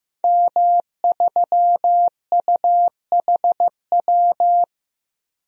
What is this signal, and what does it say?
Morse code, heard as a keyed tone, about 15 wpm, 700 Hz. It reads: M3UHW
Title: DXCC-100 Call Signs - 5 Characters - 15wpm